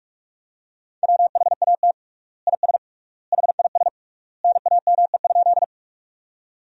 Morse code, heard as a keyed tone, about 46 wpm, 700 Hz. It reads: WHAT IS HIS NAME?
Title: 3x Word-Spacing - Sentences from Top 200 Words - 45wpm